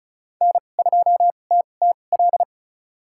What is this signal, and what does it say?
Morse code, heard as a keyed tone, about 34 wpm, spaced as for 25 wpm, 700 Hz. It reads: N2TTL
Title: USA Call Signs - 5 Characters - 2x Character Spacing - 35wpm